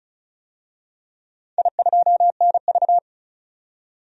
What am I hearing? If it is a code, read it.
Morse code, heard as a keyed tone, about 34 wpm, 700 Hz. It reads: I2NV
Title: DXCC-100 Call Signs - 4 Characters - 35wpm